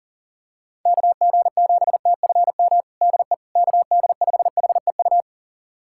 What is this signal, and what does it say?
Morse code, heard as a keyed tone, about 37 wpm, 700 Hz. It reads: KG7TFM DE KD5HEU
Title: USA Callsign DE Callsign - 6 Characters - 40wpm